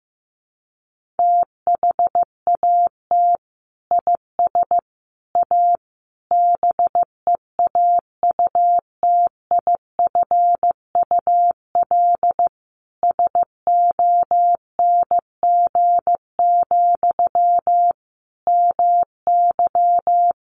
Morse code, heard as a keyed tone, about 15 wpm, 700 Hz. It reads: THAT IS A BEAUTIFUL SONG, MY
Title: Sentences from Top 1500 Words 15wpm